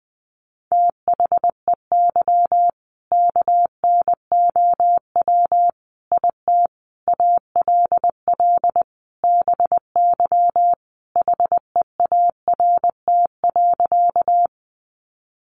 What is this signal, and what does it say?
Morse code, heard as a keyed tone, about 20 wpm, 700 Hz. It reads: THEY KNOW IT ALL BY HEART.
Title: Sentences from Top 1200 Words 20wpm